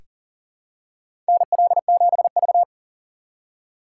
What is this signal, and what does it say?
Morse code, heard as a keyed tone, about 37 wpm, 700 Hz. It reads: NL7V
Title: K1USN SST QSO Practice - 40wpm